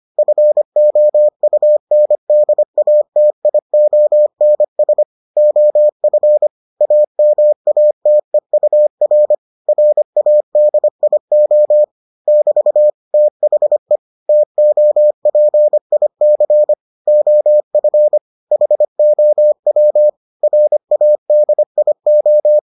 Morse code, heard as a keyed tone, about 25 wpm, 600 Hz. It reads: FOUNDATIONS OF AMATEUR RADIO <BT> THE TOPIC OF HOW RADIO